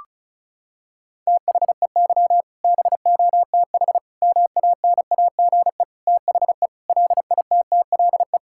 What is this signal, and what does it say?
Morse code, heard as a keyed tone, about 36 wpm, 700 Hz. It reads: THEY BOTH MANAGE THE LITTLE
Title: Sentences from Top 1600 Words 35wpm